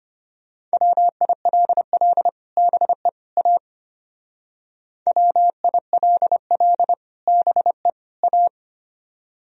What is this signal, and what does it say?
Morse code, heard as a keyed tone, about 32 wpm, 700 Hz. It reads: WILL BE A WILL BE A